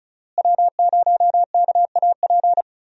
Morse code, heard as a keyed tone, about 34 wpm, 700 Hz. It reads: W0KAP
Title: USA Call Signs - 5 Characters - Repeated 3x - 35wpm